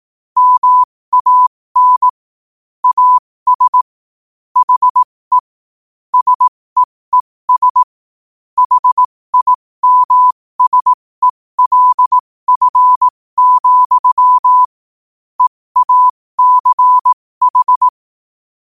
Morse code, heard as a keyed tone, about 18 wpm, spaced as for 15 wpm, 1000 Hz. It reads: MAN AS HE SEES HIMSELF, EACH
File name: qotdfiles / 15wpm